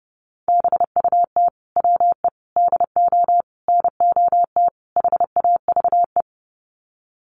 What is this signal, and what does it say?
Morse code, heard as a keyed tone, about 28 wpm, 700 Hz. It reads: BUT WE DO NOT HAVE
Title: Sets of 5 Words Encore2 30wpm